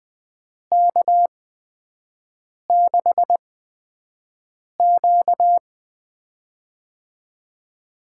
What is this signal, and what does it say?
Morse code, heard as a keyed tone, about 20 wpm, spaced as for 5 wpm, 700 Hz. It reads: K6Q
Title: USA Call Signs - 4 Characters - 8x Character Spacing - 20wpm